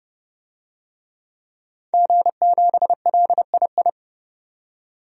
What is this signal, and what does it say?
Morse code, heard as a keyed tone, about 30 wpm, 690 Hz. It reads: G7LII
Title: DXCC-100 Call Signs - 5 Characters - 30wpm